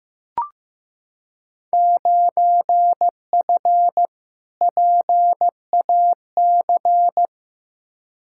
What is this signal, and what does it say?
Morse code, heard as a keyed tone, about 16 wpm, 700 Hz. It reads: 9F PAC
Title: Field Day Exchanges Encore - 15wpm